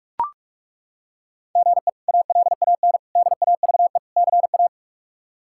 Morse code, heard as a keyed tone, about 43 wpm, 700 Hz. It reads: GE ALAN DAVE CA
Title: K1USN SST QSO Practice - 45wpm